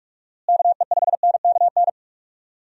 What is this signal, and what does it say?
Morse code, heard as a keyed tone, about 44 wpm, 700 Hz. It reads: KE5NKN
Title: USA Call Signs - 6 Characters - Repeated 3x - 45wpm